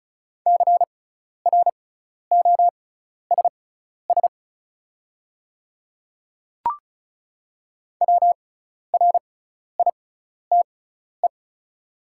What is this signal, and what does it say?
Morse code, heard as a keyed tone, about 36 wpm, spaced as for 12 wpm, 700 Hz. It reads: CROSS WRITE
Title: 6x Character Spacing - Top 300 Words - 35wpm